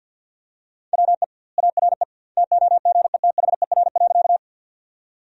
Morse code, heard as a keyed tone, about 53 wpm, 700 Hz. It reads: WE ARE TOGETHER.
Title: 2x Word-Spacing - Sentences from Top 300 Words - 50wpm